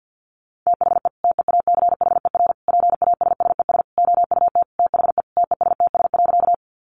Morse code, heard as a keyed tone, about 50 wpm, 700 Hz. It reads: THE TEACHER PASSES OUT THE TESTS.
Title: Additional Easy Phrases 50wpm